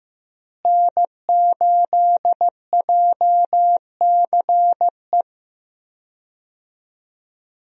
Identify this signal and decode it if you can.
Morse code, heard as a keyed tone, about 15 wpm, 700 Hz. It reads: N8JCE
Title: USA Call Signs - 5 Characters - Repeated 3x - 15wpm